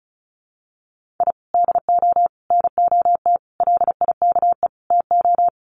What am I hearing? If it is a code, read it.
Morse code, heard as a keyed tone, about 34 wpm, 700 Hz. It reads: I DO NOT LIKE TO
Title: Sets of 5 Words 35wpm